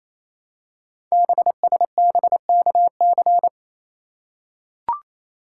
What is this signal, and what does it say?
Morse code, heard as a keyed tone, about 27 wpm, 700 Hz. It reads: BSBKC
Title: Code Groups - 5 Letters - 28wpm